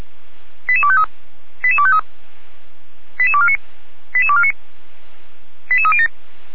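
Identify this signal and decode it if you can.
Five-tone selective call (ZVEI), heard as quick runs of five tones, tones of about 70 ms. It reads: 88253 88253 88249 88249 88307
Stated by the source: ZVEI five-tone call received by SDR, recorded by fire brigade Feuerwehr Orlamünde.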